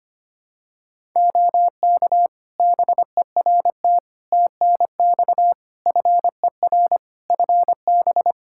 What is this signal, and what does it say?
Morse code, heard as a keyed tone, about 25 wpm, 700 Hz. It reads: OK BERT TNX FER FB
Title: Practice QSO - Exchange 4 - Ending - 25wpm